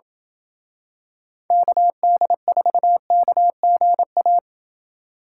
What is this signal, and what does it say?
Morse code, heard as a keyed tone, about 26 wpm, 700 Hz. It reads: KD4KGA